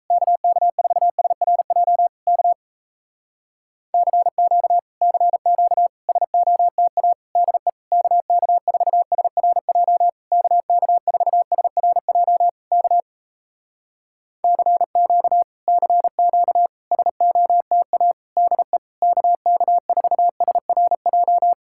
Morse code, heard as a keyed tone, about 38 wpm, 700 Hz. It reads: KK4SRJ K CQ CQ SOTA DE KK4SRJ KK4SRJ K CQ CQ SOTA DE KK4SRJ